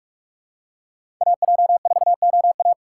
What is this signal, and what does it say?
Morse code, heard as a keyed tone, about 46 wpm, 700 Hz. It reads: AJ4OA
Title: USA Call Signs - 5 Characters - 3x Character Spacing - 45wpm